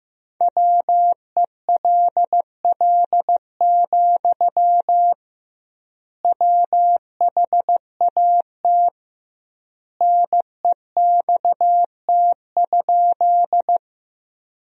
Morse code, heard as a keyed tone, about 15 wpm, 700 Hz. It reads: WELL, WHAT NEXT?
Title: 2x Word-Spacing - Sentences from Top 300 Words - 15wpm